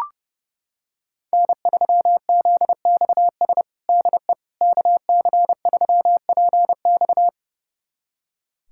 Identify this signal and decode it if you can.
Morse code, heard as a keyed tone, about 30 wpm, 700 Hz. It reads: N3ZXS DE KC3PX
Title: USA Callsign DE Callsign - 5 Characters - 30wpm